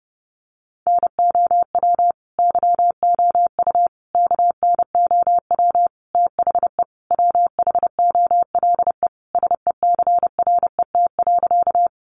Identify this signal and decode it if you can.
Morse code, heard as a keyed tone, about 30 wpm, 700 Hz. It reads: NOW YOU KNOW THE WHOLE SECRET.
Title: Sentences from Top 1200 Words 30wpm